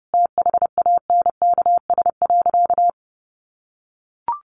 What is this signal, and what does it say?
Morse code, heard as a keyed tone, about 30 wpm, 700 Hz. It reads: THANKS.
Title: Sentences from Top 700 Words 30wpm